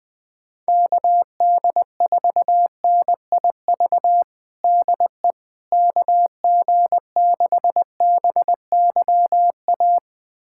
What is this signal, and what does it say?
Morse code, heard as a keyed tone, about 20 wpm, 700 Hz. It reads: KD4NIV DE KG6BYA
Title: USA Callsign DE Callsign 20wpm